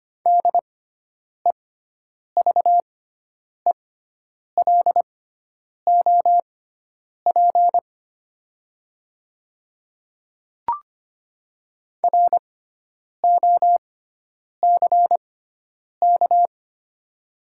Morse code, heard as a keyed tone, about 25 wpm, spaced as for 9 wpm, 700 Hz. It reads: DEVELOP ROCK
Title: 6x Character Spacing - Top 500 Words - 25wpm